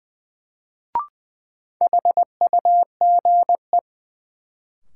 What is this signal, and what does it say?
Morse code, heard as a keyed tone, about 20 wpm, 700 Hz. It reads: HUGE
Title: Sentences from Top 1100 Words - New Word Drill - 20wpm